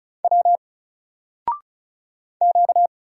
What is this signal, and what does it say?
Morse code, heard as a keyed tone, about 36 wpm, 700 Hz. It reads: W Q
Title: Single Letter-Number 35wpm